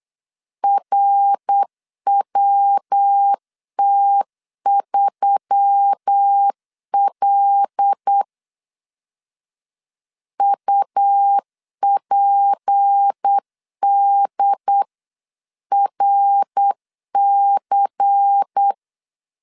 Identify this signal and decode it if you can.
Morse code, heard as a keyed tone, about 8 wpm, 800 Hz. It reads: RWT3L UPDRC
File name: LEZIONI / 6 PKX0 / 6 P 8wpm 3025